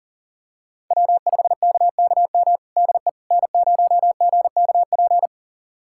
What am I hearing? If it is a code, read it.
Morse code, heard as a keyed tone, about 40 wpm, 700 Hz. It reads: W5KKM DE N0GKP
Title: USA Callsign DE Callsign - 5 Characters - 40wpm